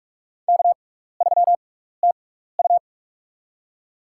Morse code, heard as a keyed tone, about 46 wpm, spaced as for 16 wpm, 700 Hz. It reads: K3TU
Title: USA Call Signs - 4 Characters - 6x Character Spacing - 45wpm